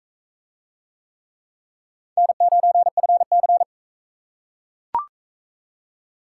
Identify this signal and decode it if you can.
Morse code, heard as a keyed tone, about 39 wpm, 700 Hz. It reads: N9FC